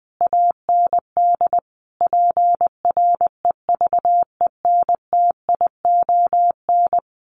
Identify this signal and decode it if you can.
Morse code, heard as a keyed tone, about 20 wpm, 700 Hz. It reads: AND PREVENTION
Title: Sets of 5 Words Encore2 20wpm